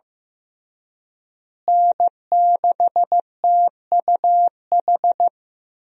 Morse code, heard as a keyed tone, about 15 wpm, 700 Hz. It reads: N6TUH
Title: USA Call Signs - 5 Characters - Repeated 3x - 15wpm